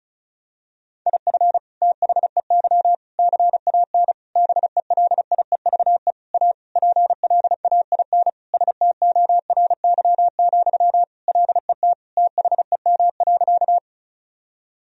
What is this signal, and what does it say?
Morse code, heard as a keyed tone, about 36 wpm, 700 Hz. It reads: IF THEY CAN BELIEVE A PLAIN STORY, LET THEM.
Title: Sentences from Top 1000 Words - Review Part 2 - 35wpm